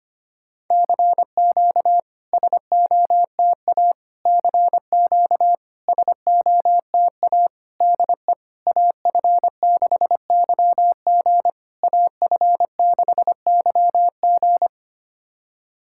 Morse code, heard as a keyed tone, about 25 wpm, 690 Hz. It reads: CQ SOTA CQ SOTA DE AF6YG AF6YG
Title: SOTA QSO Practice 25wpm